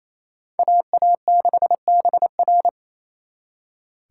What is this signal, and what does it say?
Morse code, heard as a keyed tone, about 34 wpm, 700 Hz. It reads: AA6BR
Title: Speed Racing - USA 2x2 Call Signs 28wpm